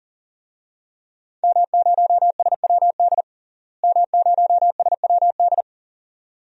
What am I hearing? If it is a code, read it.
Morse code, heard as a keyed tone, about 40 wpm, 700 Hz. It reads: M0SWD M0SWD
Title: DXCC-100 Call Signs - 5 Characters - Repeated 2x - 40wpm